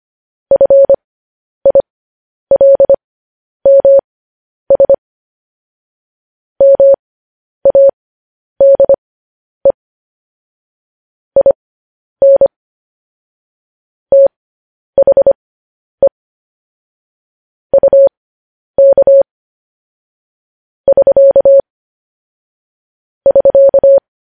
Morse code, heard as a keyed tone, about 25 wpm, spaced as for 10 wpm, 550 Hz. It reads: FILMS MADE IN THE UK <SK> <SK>
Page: news-headlines-in-morse-code-at-10-wpm